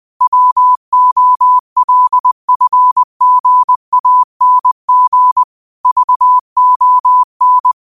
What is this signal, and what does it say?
Morse code, heard as a keyed tone, about 20 wpm, 1000 Hz. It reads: WOLFGANG VON
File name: qotdfiles / 20wpm